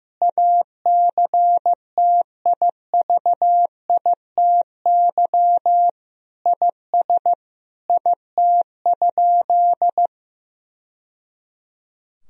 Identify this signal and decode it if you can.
Morse code, heard as a keyed tone, about 16 wpm, 700 Hz. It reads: ACTIVITY IS IT?
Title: Sentences from Top 1000 Words - Review Part 4 - 15wpm